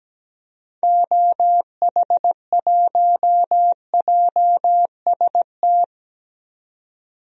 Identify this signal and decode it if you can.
Morse code, heard as a keyed tone, about 17 wpm, 700 Hz. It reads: OH1JST
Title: DXCC-100 Call Signs - 6 Characters - 17wpm